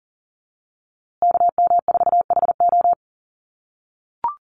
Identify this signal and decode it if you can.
Morse code, heard as a keyed tone, about 40 wpm, 700 Hz. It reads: KM4HO
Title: USA Call Signs Repeated 3x 40wpm